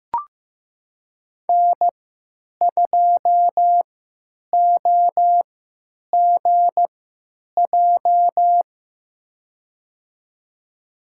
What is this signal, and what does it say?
Morse code, heard as a keyed tone, about 15 wpm, spaced as for 9 wpm, 700 Hz. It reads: N2OGJ
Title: USA Call Signs - 5 Characters - 3x Character Spacing - 15wpm